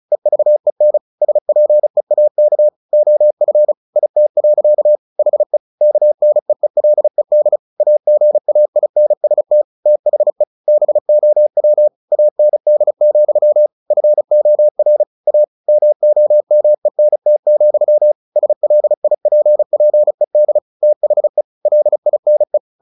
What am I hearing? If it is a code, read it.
Morse code, heard as a keyed tone, about 34 wpm, 600 Hz. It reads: EVEN SPEAK OF IT. HE KNEELED AGAINST THE BOW AND, FOR A MOMENT, SLIPPED THE LINE
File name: the old man and the sea 2 35wpm